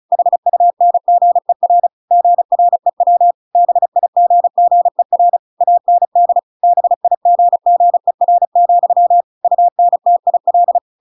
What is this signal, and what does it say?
Morse code, heard as a keyed tone, about 34 wpm, 700 Hz. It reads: HUNGER GREW BIGGER AND BIGGER, UNTIL
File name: Pinocchio / Pinocchio ch5 35 35 ik1hge